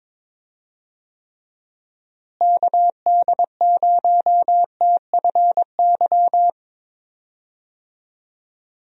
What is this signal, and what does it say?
Morse code, heard as a keyed tone, about 22 wpm, 700 Hz. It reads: KD0TFY
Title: USA Call Signs - 6 Characters - Repeated 2x - 22wpm